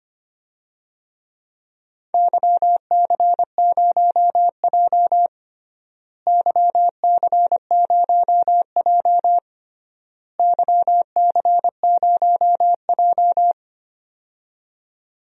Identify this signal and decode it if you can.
Morse code, heard as a keyed tone, about 25 wpm, 700 Hz. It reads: YC0J YC0J YC0J
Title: DXCC-100 Call Signs - 4 Characters - Repeated 3x - 25wpm